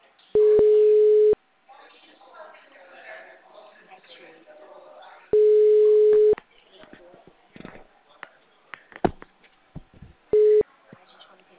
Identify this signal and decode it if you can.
Ringback tone (Europe, 425 Hz), with speech.